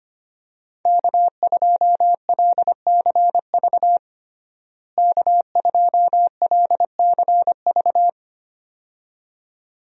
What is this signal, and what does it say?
Morse code, heard as a keyed tone, about 25 wpm, 700 Hz. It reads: K2LCV K2LCV
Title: USA Call Signs - 5 Characters - Repeated 2x - 25wpm